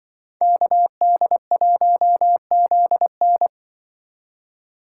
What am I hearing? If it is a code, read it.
Morse code, heard as a keyed tone, about 24 wpm, 700 Hz. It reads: KD1ZN
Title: Speed Racing - USA 2x2 Call Signs 20wpm